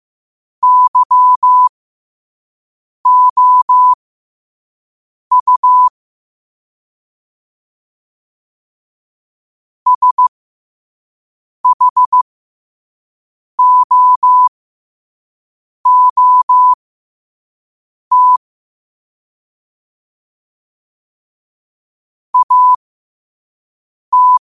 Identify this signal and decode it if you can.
Morse code, heard as a keyed tone, about 15 wpm, spaced as for 5 wpm, 1000 Hz. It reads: YOU SHOOT AT
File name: qotdfiles / 5wpm